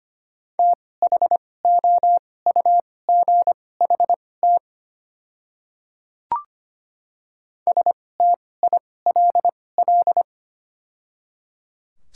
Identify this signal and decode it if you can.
Morse code, heard as a keyed tone, about 25 wpm, spaced as for 18 wpm, 700 Hz. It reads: THOUGHT STILL
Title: 2x Character Spacing - Top 300 Words - 25wpm